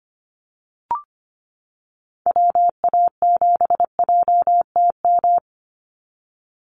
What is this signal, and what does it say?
Morse code, heard as a keyed tone, about 25 wpm, 700 Hz. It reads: WA7JTM
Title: NA SOTA Activator Callsigns 25wpm